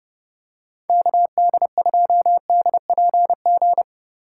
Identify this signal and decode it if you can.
Morse code, heard as a keyed tone, about 30 wpm, 700 Hz. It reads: KD2DPG
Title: USA Call Signs - 6 Characters - 4x Character Spacing - 30wpm